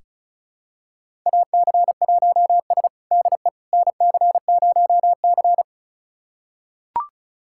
Morse code, heard as a keyed tone, about 36 wpm, 700 Hz. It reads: AC1S DE NC0C
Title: USA Callsign DE Callsign - 4 Characters - 35wpm